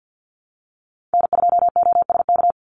Morse code, heard as a keyed tone, about 50 wpm, 700 Hz. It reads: N2OSK
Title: USA Call Signs 50wpm